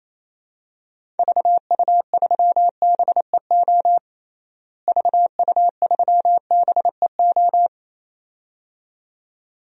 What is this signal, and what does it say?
Morse code, heard as a keyed tone, about 28 wpm, 700 Hz. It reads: VU3BEO VU3BEO
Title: DXCC-100 Call Signs - 6 Characters - Repeated 2x - 28wpm